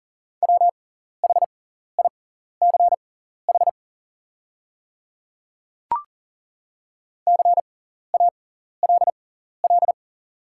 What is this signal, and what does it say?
Morse code, heard as a keyed tone, about 40 wpm, spaced as for 14 wpm, 700 Hz. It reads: WHICH CALL
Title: 6x Character Spacing - Top 100 Words - 40wpm